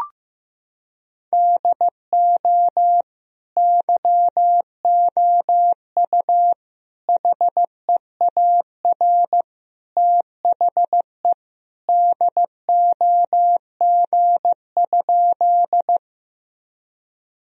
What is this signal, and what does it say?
Morse code, heard as a keyed tone, about 15 wpm, 700 Hz. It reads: DO YOU HEAR THE DOG?
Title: Sentences from Top 1000 Words - Review Part 2 - 15wpm